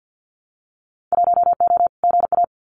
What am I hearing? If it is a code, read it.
Morse code, heard as a keyed tone, about 50 wpm, 700 Hz. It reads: 1O GA
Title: Winter Field Day Exchanges 50wpm